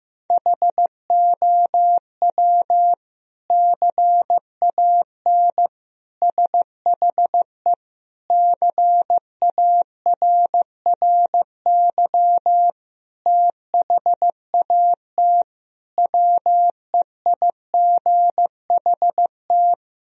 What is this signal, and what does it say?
Morse code, heard as a keyed tone, about 15 wpm, 700 Hz. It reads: HOW CAN SHE CARRY THAT WEIGHT
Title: Sentences from Top 1000 Words - Review Part 3 - 15wpm